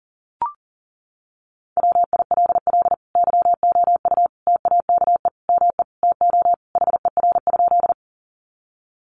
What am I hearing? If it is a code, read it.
Morse code, heard as a keyed tone, about 40 wpm, 700 Hz. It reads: WILL YOU TAKE ME TO HER?
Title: Sentences from Top 200 Words 40wpm